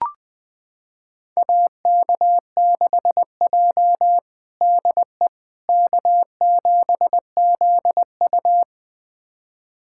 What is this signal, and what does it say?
Morse code, heard as a keyed tone, about 20 wpm, 700 Hz. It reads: AK6J DE K7ZU
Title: USA Callsign DE Callsign - 4 Characters - 20wpm